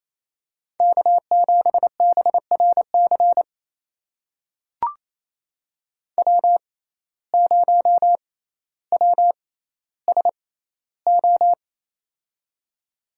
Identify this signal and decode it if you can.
Morse code, heard as a keyed tone, about 29 wpm, spaced as for 15 wpm, 700 Hz. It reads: K7BRC W0WSO
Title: USA Call Signs - 5 Characters - 6x Character Spacing - 28wpm